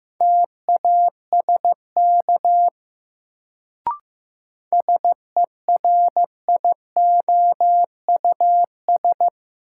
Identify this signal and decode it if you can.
Morse code, heard as a keyed tone, about 15 wpm, 700 Hz. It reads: TASK SERIOUS
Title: Sentences from Top 1100 Words - New Word Drill Alt - 15wpm